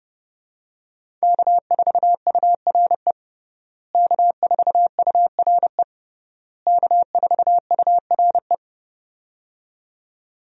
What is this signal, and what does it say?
Morse code, heard as a keyed tone, about 30 wpm, 700 Hz. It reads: K4URE K4URE K4URE
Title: USA Call Signs - 5 Characters - Repeated 3x - 30wpm